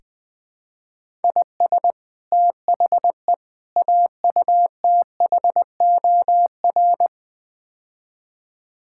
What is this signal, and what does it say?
Morse code, heard as a keyed tone, about 21 wpm, 700 Hz. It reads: IS THE AUTHOR
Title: Build A Phrase - 4-Word Phrases - 20wpm